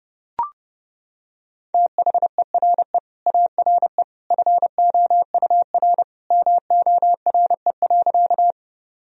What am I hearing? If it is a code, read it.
Morse code, heard as a keyed tone, about 32 wpm, 700 Hz. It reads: THERE ARE FOUR MORE.
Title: Sentences from Top 1000 Words - Review Part 1 - 30wpm